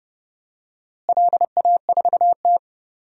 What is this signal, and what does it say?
Morse code, heard as a keyed tone, about 30 wpm, 700 Hz. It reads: LA4T
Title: DXCC-100 Call Signs - 4 Characters - Repeated 3x - 30wpm